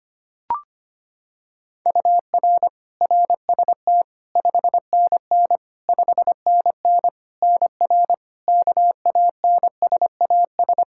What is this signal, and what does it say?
Morse code, heard as a keyed tone, about 25 wpm, 700 Hz. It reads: UR RST 5NN 5NN NR KANSAS
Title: Practice QSO - Exchange 1 - Beginning - 25wpm